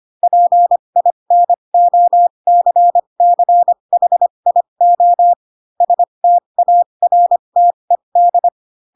Morse code, heard as a keyed tone, about 25 wpm, spaced as for 22 wpm, 700 Hz. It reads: PINOCCHIO STARTED
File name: Pinocchio / Pinocchio ch3 18 25 ik1hge